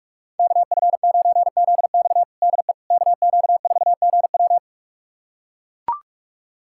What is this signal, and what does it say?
Morse code, heard as a keyed tone, about 41 wpm, 700 Hz. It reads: KF9ZX DE KQ4GW
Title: USA Callsign DE Callsign - 5 Characters - 45wpm